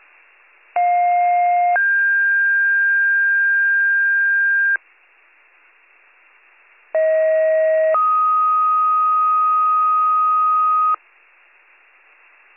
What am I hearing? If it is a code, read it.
Two-tone page, heard as a one-second tone, then a three-second tone, about 710 then 1600 Hz, about 640 then 1250 Hz. It is heard from 0.8 to 11.0 s.